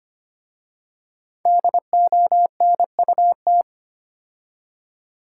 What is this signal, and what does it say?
Morse code, heard as a keyed tone, about 25 wpm, 700 Hz. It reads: DONUT
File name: practice / 25wpm / Difficult Words with Hints - 25wpm